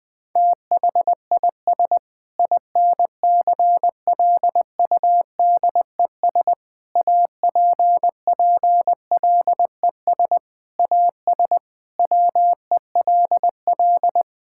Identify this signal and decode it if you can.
Morse code, heard as a keyed tone, about 20 wpm, 700 Hz. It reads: THIS INCLUDES APPLES AS WELL
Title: Sentences with 7-Word Phrases - Spoken Beginning - 20wpm